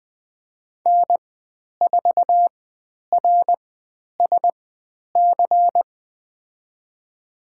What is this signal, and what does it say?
Morse code, heard as a keyed tone, about 20 wpm, spaced as for 10 wpm, 700 Hz. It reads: N4RSC
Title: 095 Lesson 16g - Letter c - Call Signs 20wpm